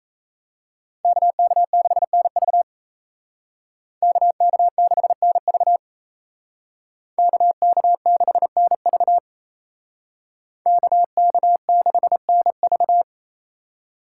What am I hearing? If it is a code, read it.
Morse code, heard as a keyed tone, about 36 wpm, 700 Hz. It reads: KK6NV KK6NV KK6NV KK6NV